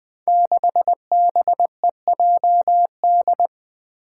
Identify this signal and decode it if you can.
Morse code, heard as a keyed tone, about 20 wpm, 700 Hz. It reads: 6BEJD
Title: Five Challenging Letter-Number Combinations 20wpm